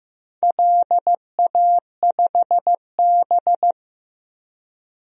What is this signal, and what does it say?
Morse code, heard as a keyed tone, about 15 wpm, 700 Hz. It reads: LA5B
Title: DXCC-100 Call Signs - 4 Characters - Repeated 2x - 15wpm